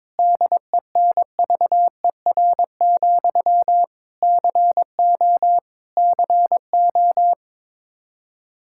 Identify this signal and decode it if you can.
Morse code, heard as a keyed tone, about 21 wpm, 700 Hz. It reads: DENVER, CO CO
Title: Practice QSO - Exchange 1 - Beginning - 22wpm